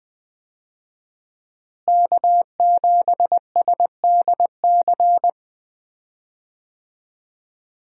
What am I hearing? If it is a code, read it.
Morse code, heard as a keyed tone, about 21 wpm, 700 Hz. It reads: K7SDC
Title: USA Call Signs - 5 Characters - Repeated 3x - 20wpm